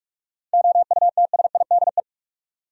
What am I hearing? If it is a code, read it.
Morse code, heard as a keyed tone, about 46 wpm, 700 Hz. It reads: OUTSIDE
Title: Sentences from Top 900 Words - New Word Drill - 45wpm